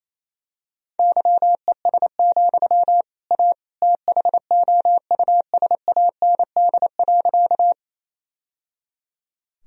Morse code, heard as a keyed tone, about 28 wpm, 700 Hz. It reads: YES, A THOUSAND.
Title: Sentences from Top 1000 Words - Review Part 3 - 28wpm